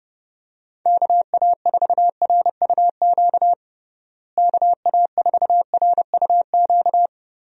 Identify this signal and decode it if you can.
Morse code, heard as a keyed tone, about 32 wpm, 700 Hz. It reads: KA4RUQ KA4RUQ
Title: USA Call Signs - 6 Characters - Repeated 2x - 30wpm